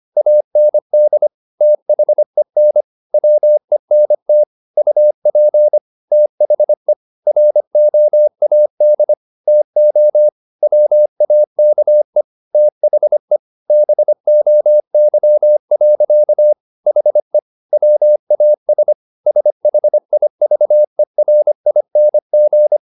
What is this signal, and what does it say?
Morse code, heard as a keyed tone, about 25 wpm, 600 Hz. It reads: AND THEN WENT UP THE ROAD TO WAKE THE BOY. HE WAS SHIVERING
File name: the old man and the sea part 1 25wpm